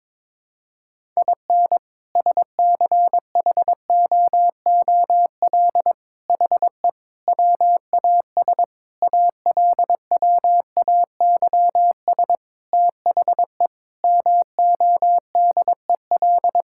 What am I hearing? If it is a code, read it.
Morse code, heard as a keyed tone, about 22 wpm, 700 Hz. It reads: IN SCHOOL HE WAS ALWAYS THE MODEL
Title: Sentences from Top 1000 Words - Review Part 5 - 22wpm